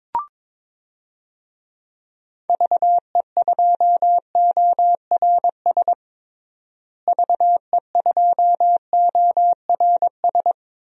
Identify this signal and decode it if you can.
Morse code, heard as a keyed tone, about 22 wpm, 700 Hz. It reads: VE2ORS VE2ORS
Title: DXCC-100 Call Signs - 6 Characters - Repeated 3x - 22wpm